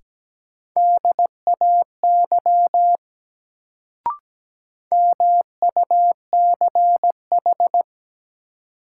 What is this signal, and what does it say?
Morse code, heard as a keyed tone, about 17 wpm, 700 Hz. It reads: DAY MUCH
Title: Sentences from Top 200 Words - New Word Drill - 17wpm